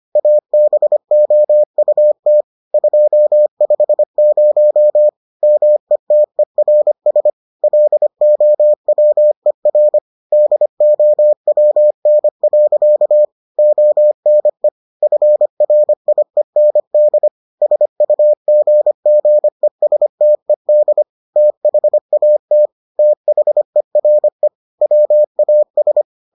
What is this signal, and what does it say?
Morse code, heard as a keyed tone, about 25 wpm, 600 Hz. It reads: ABOUT 250 METERS LOWER DOWN. ONE FRIEND SUGGESTED THAT THERE WAS